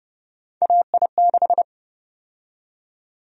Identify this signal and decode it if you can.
Morse code, heard as a keyed tone, about 30 wpm, 700 Hz. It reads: AI6
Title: USA Call Sign Prefixes - Repeated 2x - 30wpm